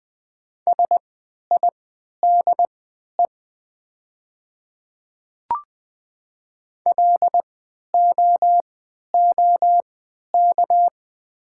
Morse code, heard as a keyed tone, about 21 wpm, spaced as for 12 wpm, 700 Hz. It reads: SIDE LOOK
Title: 3x Character Spacing - Top 100 Words - 20wpm